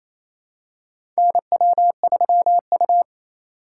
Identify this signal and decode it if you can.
Morse code, heard as a keyed tone, about 28 wpm, 700 Hz. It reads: NW3U
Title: USA Call Signs - 4 Characters - 8x Character Spacing - 28wpm